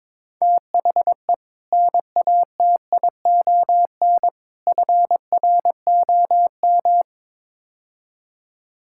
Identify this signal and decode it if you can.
Morse code, heard as a keyed tone, about 21 wpm, 700 Hz. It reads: THE NATION FROM
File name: practice / 22wpm / Build A Phrase - 5-Word Phrases - 22wpm